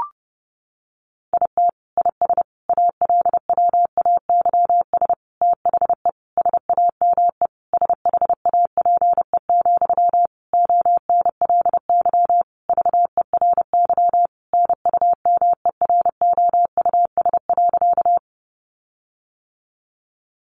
Morse code, heard as a keyed tone, about 30 wpm, 700 Hz. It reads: IT IS ALWAYS THE SAME SHAPE, ONLY VERY NUMEROUS.
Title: Sentences from Top 1400 Words 30wpm